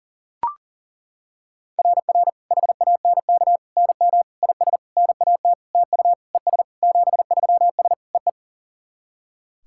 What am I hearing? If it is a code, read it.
Morse code, heard as a keyed tone, about 40 wpm, 700 Hz. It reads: RR HANK NM IS NAT TU ES 73S EE
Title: SOTA QSO Practice 40wpm